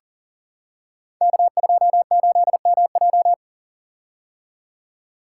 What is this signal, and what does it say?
Morse code, heard as a keyed tone, about 38 wpm, 700 Hz. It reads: K28MJ
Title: Code Groups - 5 Letter:Numbers - 40wpm